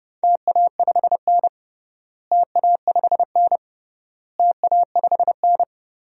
Morse code, heard as a keyed tone, about 32 wpm, 700 Hz. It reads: TA5N TA5N TA5N
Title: DXCC-100 Call Signs - 4 Characters - Repeated 3x - 30wpm